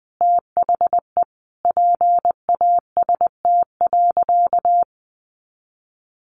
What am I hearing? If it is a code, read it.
Morse code, heard as a keyed tone, about 19 wpm, 700 Hz. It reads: THE PAST.
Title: A Sound Of Thunder 20wpm